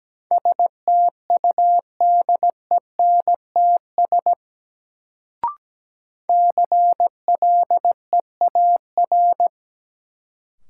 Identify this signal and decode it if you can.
Morse code, heard as a keyed tone, about 17 wpm, 700 Hz. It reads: STUDENTS CLEAR
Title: Sets of 5 Words Encore - New Word Drill - 17wpm